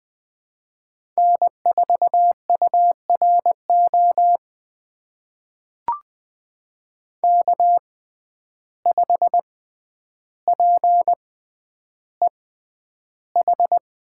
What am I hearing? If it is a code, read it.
Morse code, heard as a keyed tone, about 20 wpm, spaced as for 10 wpm, 700 Hz. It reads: N4URO K5PEH
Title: USA Call Signs - 5 Characters - 6x Character Spacing - 20wpm